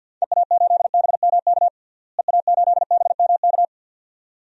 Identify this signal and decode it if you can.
Morse code, heard as a keyed tone, about 53 wpm, 700 Hz. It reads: EA8BMK EA8BMK
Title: DXCC-100 Call Signs - 6 Characters - Repeated 2x - 50wpm